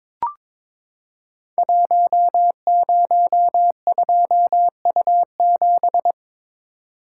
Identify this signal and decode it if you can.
Morse code, heard as a keyed tone, about 22 wpm, 700 Hz. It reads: 102U7
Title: Code Groups - 5 Letter:Numbers - 22wpm